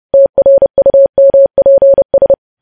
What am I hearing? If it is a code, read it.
Morse code, heard as a keyed tone, about 30 wpm, 550 Hz. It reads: TRUMPS
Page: news headlines in morse code at 30 wpm